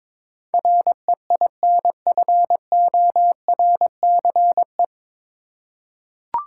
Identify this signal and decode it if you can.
Morse code, heard as a keyed tone, about 22 wpm, 700 Hz. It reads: REINFORCE
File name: 9 Letter Words - First 3 Letters Given - 22wpm